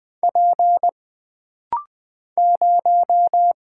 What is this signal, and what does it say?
Morse code, heard as a keyed tone, about 20 wpm, 700 Hz. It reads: P 0
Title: Single Letter-Number - Rapid-Fire 20wpm